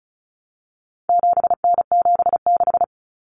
Morse code, heard as a keyed tone, about 34 wpm, 700 Hz. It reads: 7N76
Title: Common Cut Numbers - 4-Digit 35wpm